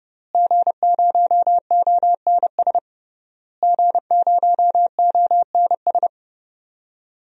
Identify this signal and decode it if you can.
Morse code, heard as a keyed tone, about 30 wpm, 700 Hz. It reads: G0ONS G0ONS
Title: DXCC-100 Call Signs - 5 Characters - Repeated 3x - 30wpm